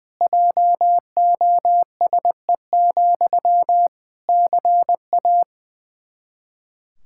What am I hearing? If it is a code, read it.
Morse code, heard as a keyed tone, about 21 wpm, 700 Hz. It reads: JOSE, CA
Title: Practice QSO - Exchange 1 - Beginning - 20wpm